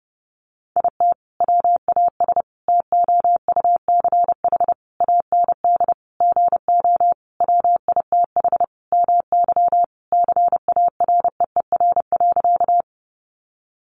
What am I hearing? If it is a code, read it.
Morse code, heard as a keyed tone, about 30 wpm, 700 Hz. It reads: IT WAS TOUCH AND GO WITH MY CAREER.
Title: Sentences from Top 1200 Words 30wpm